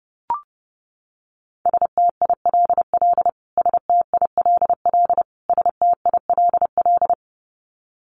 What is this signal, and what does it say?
Morse code, heard as a keyed tone, about 30 wpm, 700 Hz. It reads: STILL STILL STILL
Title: Top 500 Words Repeated 30wpm